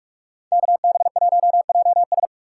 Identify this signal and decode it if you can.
Morse code, heard as a keyed tone, about 42 wpm, 690 Hz. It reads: KB1JS
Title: USA Call Signs - 5 Characters - 2x Character Spacing - 45wpm